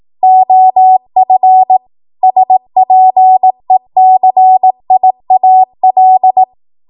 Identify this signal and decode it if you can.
Morse code, heard as a keyed tone, about 18 wpm, 750 Hz. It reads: OF SPECIAL
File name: Morse / 18WPM